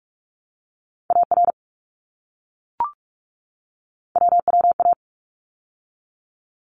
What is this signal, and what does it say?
Morse code, heard as a keyed tone, about 44 wpm, 700 Hz. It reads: AR WWA
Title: ARRL Sections 45wpm